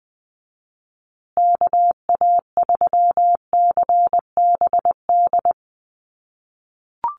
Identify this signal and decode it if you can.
Morse code, heard as a keyed tone, about 20 wpm, 700 Hz. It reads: KA3CBD
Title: USA Call Signs Repeated 3x 20wpm